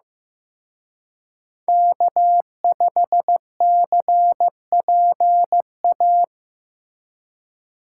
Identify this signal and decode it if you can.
Morse code, heard as a keyed tone, about 18 wpm, 700 Hz. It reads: K5CPA
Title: Speed Racing - USA Call Signs 15wpm